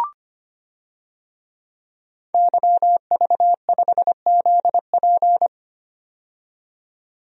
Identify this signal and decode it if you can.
Morse code, heard as a keyed tone, about 25 wpm, 700 Hz. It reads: YV5ZP
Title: DXCC-100 Call Signs - 5 Characters - 25wpm